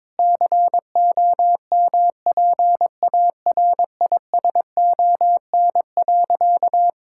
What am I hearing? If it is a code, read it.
Morse code, heard as a keyed tone, about 22 wpm, 700 Hz. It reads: COMPARISON.
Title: Sentences from Top 1600 Words 22wpm